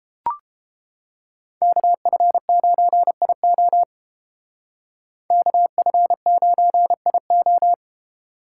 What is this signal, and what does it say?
Morse code, heard as a keyed tone, about 29 wpm, 700 Hz. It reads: KF9IO KF9IO